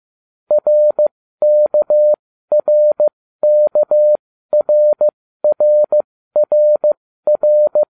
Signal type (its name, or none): Morse code